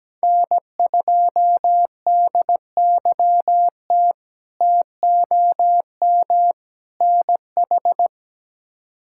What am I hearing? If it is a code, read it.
Morse code, heard as a keyed tone, about 17 wpm, 700 Hz. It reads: N2DYT TOM NH
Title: K1USN SST QSO Practice - 17wpm